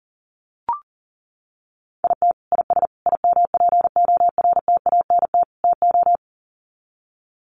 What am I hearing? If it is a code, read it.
Morse code, heard as a keyed tone, about 40 wpm, 700 Hz. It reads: IT IS IMPORTANT TO
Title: Sets of 4 Words 40wpm